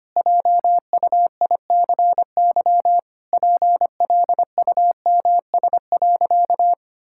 Morse code, heard as a keyed tone, about 25 wpm, 700 Hz. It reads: JUICY PLUMS.
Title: Pangram Challenge v2 - 25wpm